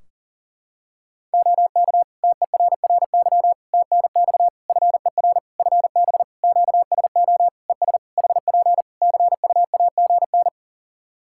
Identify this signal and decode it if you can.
Morse code, heard as a keyed tone, about 40 wpm, 700 Hz. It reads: OK TERRY TNX FER FB QSO ES HP CUAGN
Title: Practice QSO - Exchange 4 - Ending - 40wpm